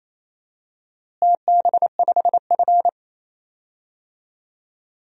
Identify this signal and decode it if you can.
Morse code, heard as a keyed tone, about 28 wpm, 700 Hz. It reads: TB5F
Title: DXCC-100 Call Signs - 4 Characters - Repeated 3x - 28wpm